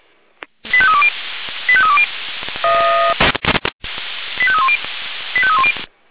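Five-tone selective call (ZVEI), heard as quick runs of five tones, tones of about 70 ms. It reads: 85320 85320 85310 85310
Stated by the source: ZVEI five-tone selective call received on an SDR.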